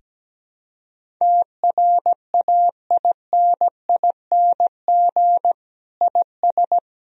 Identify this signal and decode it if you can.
Morse code, heard as a keyed tone, about 17 wpm, 700 Hz. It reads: TRAINING IS
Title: Sentences from Top 1000 Words - Review Part 4 - 17wpm